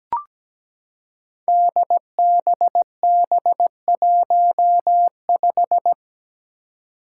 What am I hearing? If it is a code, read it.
Morse code, heard as a keyed tone, about 17 wpm, 700 Hz. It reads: DBB15
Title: Five Challenging Letter-Number Combinations 17wpm